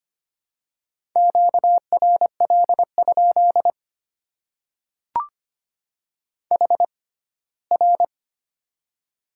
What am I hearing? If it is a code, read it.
Morse code, heard as a keyed tone, about 25 wpm, spaced as for 17 wpm, 700 Hz. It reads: QRL? HR
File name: Top 90 QSO Elements - 6x Character Spacing - 25wpm